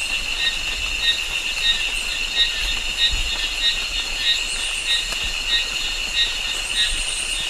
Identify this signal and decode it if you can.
There is no code.